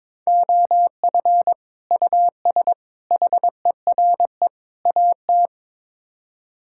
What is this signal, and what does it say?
Morse code, heard as a keyed tone, about 22 wpm, 700 Hz. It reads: OF US HERE AT
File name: practice / 22wpm / Build A Phrase - 5-Word Phrases - 22wpm